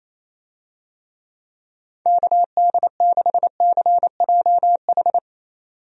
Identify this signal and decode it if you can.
Morse code, heard as a keyed tone, about 28 wpm, 700 Hz. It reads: KD6CJH
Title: USA Call Signs - 6 Characters - 28wpm